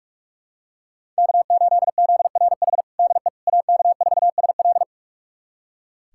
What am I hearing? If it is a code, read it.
Morse code, heard as a keyed tone, about 46 wpm, 700 Hz. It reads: K8ZRH DE AK4SL
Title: USA Callsign DE Callsign - 5 Characters - 45wpm